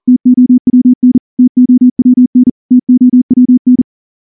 Morse code, heard as a keyed tone, about 38 wpm, 250 Hz. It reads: TOWN TOWN TOWN
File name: morse-training-40WPM